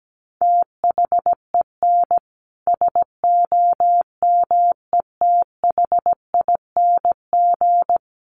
Morse code, heard as a keyed tone, about 17 wpm, 700 Hz. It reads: THEN SOMETHING
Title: Sentences from Top 1100 Words 17wpm